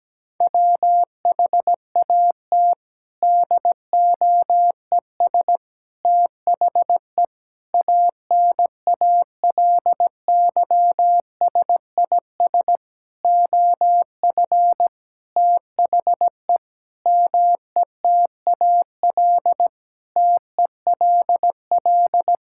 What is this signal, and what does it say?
Morse code, heard as a keyed tone, about 17 wpm, 700 Hz. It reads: WHAT DOES THE ANALYSIS OF THE METAL TELL
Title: Sentences from Top 1000 Words - Review Part 6 - 17wpm